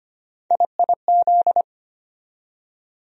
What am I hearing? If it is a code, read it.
Morse code, heard as a keyed tone, about 25 wpm, 700 Hz. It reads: IIZ
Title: USA Call Sign Postfixes - 3 Letters 25wpm